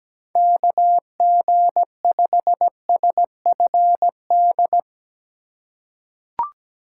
Morse code, heard as a keyed tone, about 17 wpm, 700 Hz. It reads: KG5SFD
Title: USA Call Signs - 6 Characters - 4x Character Spacing - 17wpm